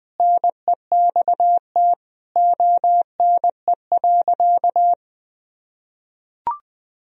Morse code, heard as a keyed tone, about 19 wpm, 700 Hz. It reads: NEXT ONE.
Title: Sentences from Top 400 Words and Abbreviations 20wpm